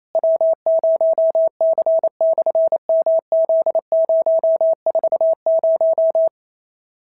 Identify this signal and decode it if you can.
Morse code, heard as a keyed tone, about 28 wpm, 650 Hz. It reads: W0C/MZ040
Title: SOTA QSO Practice 28wpm